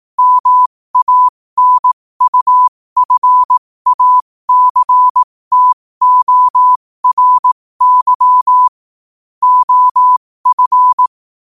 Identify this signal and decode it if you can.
Morse code, heard as a keyed tone, about 18 wpm, spaced as for 15 wpm, 1000 Hz. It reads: MANUFACTORY OF
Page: New quotes every day in morse code at 15 Words per minute.